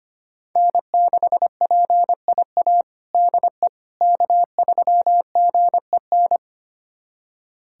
Morse code, heard as a keyed tone, about 25 wpm, 700 Hz. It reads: N6PIA DE K3GEN
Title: USA Callsign DE Callsign - 5 Characters - 25wpm